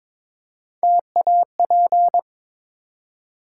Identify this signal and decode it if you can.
Morse code, heard as a keyed tone, about 23 wpm, 700 Hz. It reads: TAP
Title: Three Letter Words - 22wpm